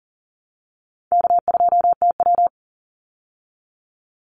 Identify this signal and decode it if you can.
Morse code, heard as a keyed tone, about 40 wpm, 700 Hz. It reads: K2TW
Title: Easy Call Signs Repeated 3x - 40wpm